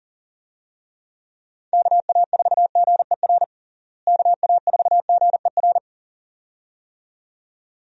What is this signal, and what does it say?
Morse code, heard as a keyed tone, about 40 wpm, 700 Hz. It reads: KA4GER KA4GER
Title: USA Call Signs - 6 Characters - Repeated 2x - 40wpm